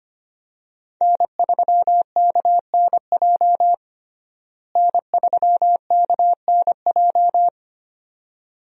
Morse code, heard as a keyed tone, about 25 wpm, 700 Hz. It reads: N3KNJ N3KNJ
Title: USA Call Signs - 5 Characters - Repeated 2x - 25wpm